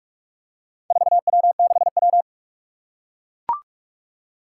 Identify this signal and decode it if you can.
Morse code, heard as a keyed tone, about 44 wpm, 700 Hz. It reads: 4W6W
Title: Four Challenging Letter-Number Combinations 45wpm